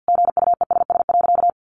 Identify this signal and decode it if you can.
Morse code, heard as a keyed tone, about 50 wpm, 700 Hz. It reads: GUESS.
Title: Sentences from Top 1500 Words 50wpm